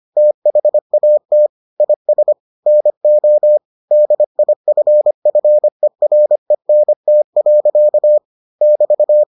Morse code, heard as a keyed tone, about 25 wpm, 600 Hz. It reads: THAT IS NO DIFFERENT. <BT>